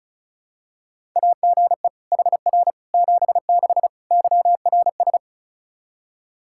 Morse code, heard as a keyed tone, about 36 wpm, 700 Hz. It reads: AGE HR 76 YRS
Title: Practice QSO - Exchange 3 - More Info - 35wpm